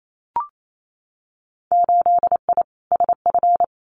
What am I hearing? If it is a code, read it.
Morse code, heard as a keyed tone, about 28 wpm, 700 Hz. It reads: 8I SF
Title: Winter Field Day Exchanges Encore 28wpm